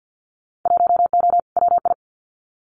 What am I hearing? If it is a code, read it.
Morse code, heard as a keyed tone, about 50 wpm, 700 Hz. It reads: 1O WI
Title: Winter Field Day Exchanges 50wpm